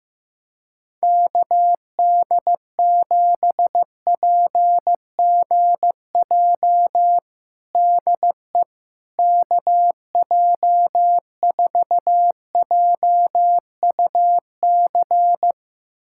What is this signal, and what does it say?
Morse code, heard as a keyed tone, about 15 wpm, 700 Hz. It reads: KD7PGJ DE KJ4JUC
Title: USA Callsign DE Callsign 15wpm